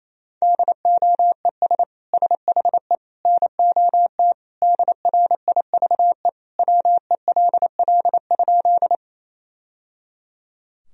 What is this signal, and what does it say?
Morse code, heard as a keyed tone, about 28 wpm, 700 Hz. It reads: DOES SHE NOT DRIVE WELL?
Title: Sentences from Top 1000 Words - Review Part 2 - 28wpm